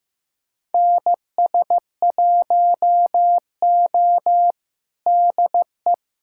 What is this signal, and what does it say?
Morse code, heard as a keyed tone, about 15 wpm, 700 Hz. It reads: NS1O DE
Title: USA Callsign DE Callsign - 4 Characters - 15wpm